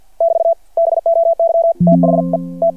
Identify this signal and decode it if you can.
Morse code, heard as a keyed tone, about 50 wpm, 670 Hz. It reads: <BT> BOY THE T
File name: archived / CRSnet